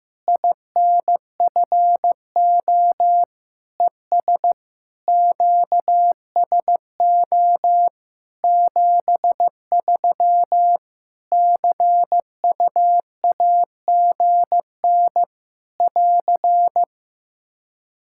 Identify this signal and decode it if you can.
Morse code, heard as a keyed tone, about 15 wpm, 700 Hz. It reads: INFO ES QSO 73 CUAGN <AR>
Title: Practice QSO - Exchange 4 - Ending - 15wpm